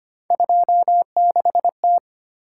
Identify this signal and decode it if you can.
Morse code, heard as a keyed tone, about 25 wpm, 700 Hz. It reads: 26T
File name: Three Letter-Number Combinations - 25wpm